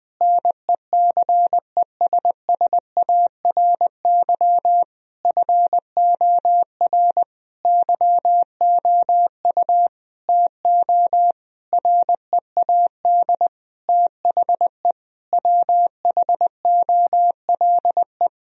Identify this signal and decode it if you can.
Morse code, heard as a keyed tone, about 21 wpm, 700 Hz. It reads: NECESSARY FOR YOU TO READ THE WHOLE
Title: Sentences from Top 1000 Words - Review Part 1 - 20wpm